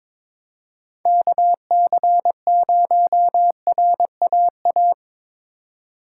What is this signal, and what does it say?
Morse code, heard as a keyed tone, about 25 wpm, 700 Hz. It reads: KC0RAA
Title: Speed Racing - USA 2x3 Call Signs 22wpm